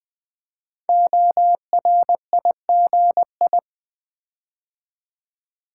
Morse code, heard as a keyed tone, about 20 wpm, 700 Hz. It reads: ORIGI
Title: Build A Word - 5 Letters At A Time - 10-Letter Words - 20wpm